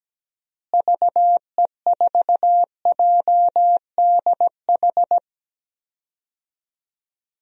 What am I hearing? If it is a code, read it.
Morse code, heard as a keyed tone, about 17 wpm, 700 Hz. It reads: VE4JDH
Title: DXCC-100 Call Signs - 6 Characters - Repeated 2x - 17wpm